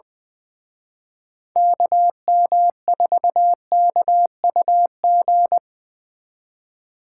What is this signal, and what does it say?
Morse code, heard as a keyed tone, about 24 wpm, 700 Hz. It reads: KM4KUG
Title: Speed Racing - USA 2x3 Call Signs 20wpm